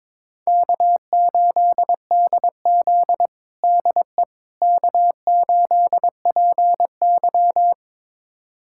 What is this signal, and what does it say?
Morse code, heard as a keyed tone, about 21 wpm, 700 Hz. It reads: K8DZ DE K8PY
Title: USA Callsign DE Callsign - 4 Characters - 22wpm